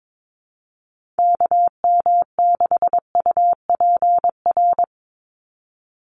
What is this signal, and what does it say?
Morse code, heard as a keyed tone, about 22 wpm, 700 Hz. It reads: KM6UPR
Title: USA Call Signs 22wpm